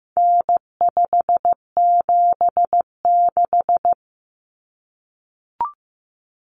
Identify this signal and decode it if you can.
Morse code, heard as a keyed tone, about 15 wpm, 700 Hz. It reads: N576
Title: Common Cut Numbers - 4-Digit 15wpm